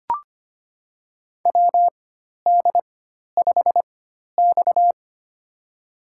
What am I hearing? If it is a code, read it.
Morse code, heard as a keyed tone, about 25 wpm, spaced as for 12 wpm, 700 Hz. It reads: WD5X
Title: USA Call Signs - 4 Characters - 4x Character Spacing - 25wpm